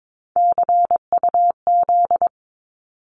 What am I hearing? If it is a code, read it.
Morse code, heard as a keyed tone, about 22 wpm, 700 Hz. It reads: CUZ
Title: Abbreviations 22wpm